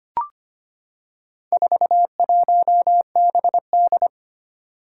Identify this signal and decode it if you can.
Morse code, heard as a keyed tone, about 25 wpm, 700 Hz. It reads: 41BD
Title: Four Challenging Letter-Number Combinations 25wpm